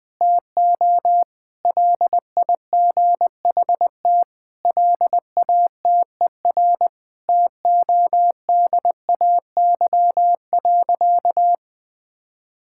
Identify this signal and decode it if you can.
Morse code, heard as a keyed tone, about 20 wpm, 700 Hz. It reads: TO LIGHT LATER TODAY.
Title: Sentences from Top 1000 Words - Review Part 6 - 20wpm